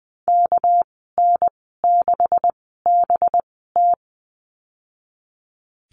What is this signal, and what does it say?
Morse code, heard as a keyed tone, about 20 wpm, spaced as for 15 wpm, 700 Hz. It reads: KN6BT
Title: 207 Lesson 33g - Letter j - Call Signs 20wpm